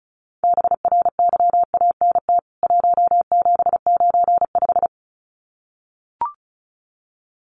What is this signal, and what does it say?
Morse code, heard as a keyed tone, about 34 wpm, 700 Hz. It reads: BRYANT 1795
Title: CWT Exchange 35wpm